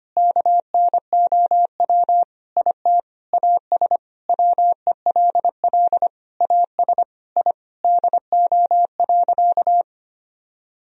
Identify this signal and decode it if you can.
Morse code, heard as a keyed tone, about 25 wpm, 700 Hz. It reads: KNOW IT AS WELL AS I DO.
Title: Sentences from Top 1000 Words - Review Part 3 - 25wpm